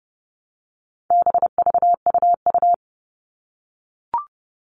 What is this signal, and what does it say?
Morse code, heard as a keyed tone, about 30 wpm, 700 Hz. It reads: BVUU
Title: All Cut Numbers - 4-Digit 30wpm